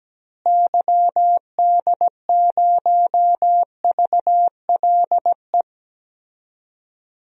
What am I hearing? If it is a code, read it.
Morse code, heard as a keyed tone, about 17 wpm, 700 Hz. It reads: YD0VLE
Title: DXCC-100 Call Signs - 6 Characters - Repeated 3x - 17wpm